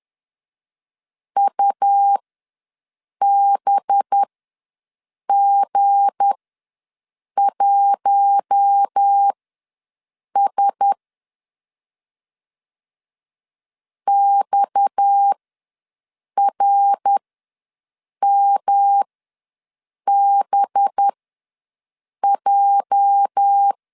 Morse code, heard as a keyed tone, about 10 wpm, spaced as for 6 wpm, 800 Hz. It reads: UBG1S XRMBJ
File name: LEZIONI / 8 YJ45 / 8 YJ ws 10wpm 4043